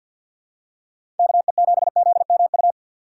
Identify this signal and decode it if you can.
Morse code, heard as a keyed tone, about 53 wpm, 690 Hz. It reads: KE7ZMU
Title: USA Call Signs - 2x Character Spacing - 50wpm